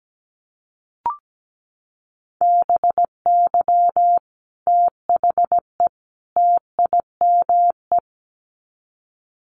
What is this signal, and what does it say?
Morse code, heard as a keyed tone, about 17 wpm, 700 Hz. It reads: BY THE TIME
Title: Sets of 3 Words 17wpm